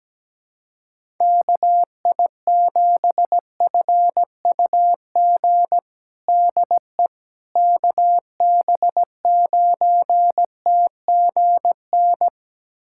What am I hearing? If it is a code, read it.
Morse code, heard as a keyed tone, about 17 wpm, 700 Hz. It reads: KI7FUG DE KB9TGN
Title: USA Callsign DE Callsign - 6 Characters - 17wpm